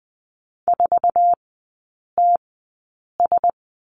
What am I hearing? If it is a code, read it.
Morse code, heard as a keyed tone, about 20 wpm, spaced as for 8 wpm, 700 Hz. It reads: 4TS
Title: 133 Lesson 22e - Question Mark - Three Characters 20wpm